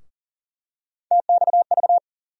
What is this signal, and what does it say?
Morse code, heard as a keyed tone, about 40 wpm, 700 Hz. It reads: TXV
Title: USA Call Sign Postfixes - 3 Letters 40wpm